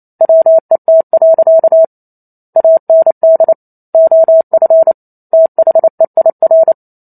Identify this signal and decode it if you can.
Morse code, heard as a keyed tone, about 28 wpm, 650 Hz. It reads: WET. AND OF THEIR
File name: Chapter 2 - 28 WPM